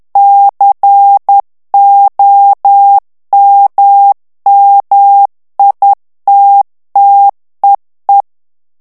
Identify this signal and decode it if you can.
Morse code, heard as a keyed tone, about 10 wpm, 800 Hz. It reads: COMMITTEE
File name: ITALIEN texte clair 2 vitesse 10 mots minute